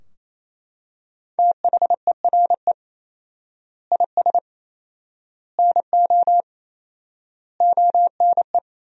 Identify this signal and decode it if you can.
Morse code, heard as a keyed tone, about 28 wpm, 700 Hz. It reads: THERE IS NO ONE
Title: 4x Word-Spacing - Sentences from Top 200 Words - 28wpm